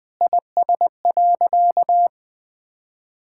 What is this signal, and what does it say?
Morse code, heard as a keyed tone, about 20 wpm, 700 Hz. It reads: IS.
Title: Sentences with 5-Word Phrases - 20wpm